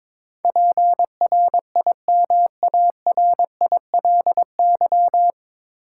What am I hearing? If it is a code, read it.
Morse code, heard as a keyed tone, about 22 wpm, 700 Hz. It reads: PRIMARILY
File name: 9 Letter Words - First 3 Letters Given - 22wpm